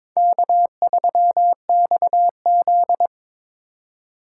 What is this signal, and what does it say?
Morse code, heard as a keyed tone, about 22 wpm, 700 Hz. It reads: K3XZ
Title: USA Call Signs - 4 Characters - 4x Character Spacing - 22wpm